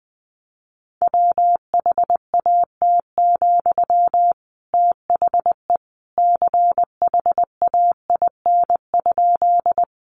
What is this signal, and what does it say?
Morse code, heard as a keyed tone, about 20 wpm, 700 Hz. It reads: WHAT, THE CHAIN?
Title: Sentences from Top 1500 Words 20wpm